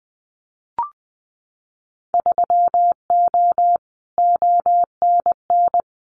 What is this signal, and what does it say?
Morse code, heard as a keyed tone, about 20 wpm, 700 Hz. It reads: 3O ONN
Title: Winter Field Day Exchanges 20wpm